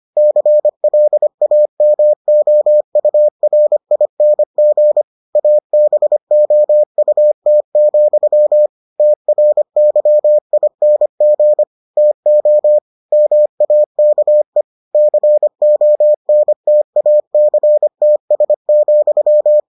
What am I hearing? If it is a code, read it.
Morse code, heard as a keyed tone, about 25 wpm, 600 Hz. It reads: CLAMOURING ABOUT, TRYING TO MAKE CONTACTS,